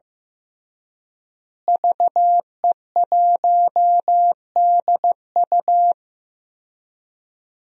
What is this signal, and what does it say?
Morse code, heard as a keyed tone, about 15 wpm, 700 Hz. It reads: VE1DU
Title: DXCC-100 Call Signs - 5 Characters - Repeated 2x - 15wpm